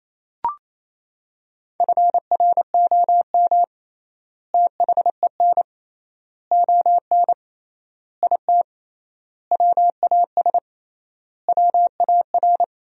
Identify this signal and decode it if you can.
Morse code, heard as a keyed tone, about 28 wpm, 700 Hz. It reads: FROM THEN ON IT WAS WAR
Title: 3x Word-Spacing - Sentences from Top 400 Words - 28wpm